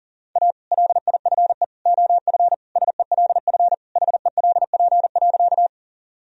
Morse code, heard as a keyed tone, about 40 wpm, 700 Hz. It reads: A LIFE OF SELF HELP.
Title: Sentences from Top 1000 Words - Review Part 1 - 40wpm